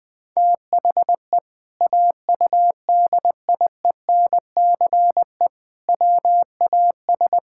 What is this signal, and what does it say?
Morse code, heard as a keyed tone, about 20 wpm, 700 Hz. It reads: THE AUDIENCE WAS
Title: Sentences from Top 1000 Words - Review Part 5 - 20wpm